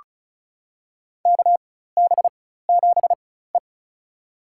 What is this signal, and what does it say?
Morse code, heard as a keyed tone, about 36 wpm, spaced as for 16 wpm, 700 Hz. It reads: KB7E
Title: USA Call Signs - 4 Characters - 4x Character Spacing - 35wpm